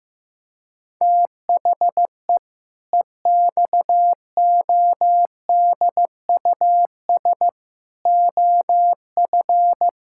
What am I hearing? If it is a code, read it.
Morse code, heard as a keyed tone, about 15 wpm, 700 Hz. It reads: THE EXODUS OF
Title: Pangram Challenge v2 - 15wpm